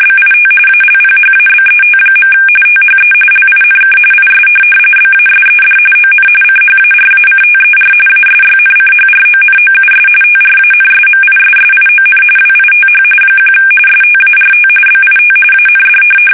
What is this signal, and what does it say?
RTTY, about 45.45 baud. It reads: BIT ALPHABET. THE LINK IS BASED ON CHARACTER ASYNCHRONOUS TRANSMISSION WITH 1 START BIT AND
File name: fsk850 / teletype-RTTY